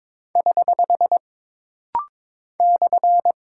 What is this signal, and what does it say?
Morse code, heard as a keyed tone, about 22 wpm, 700 Hz. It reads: <HH> /
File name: practice / 22wpm / Single Punctuation-Prosign - Rapid-Fire - 22wpm